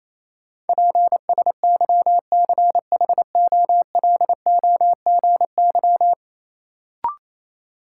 Morse code, heard as a keyed tone, about 28 wpm, 700 Hz. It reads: PSYCHOLOGY
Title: Sentences from Top 1400 Words - New Word Drill Alt - 28wpm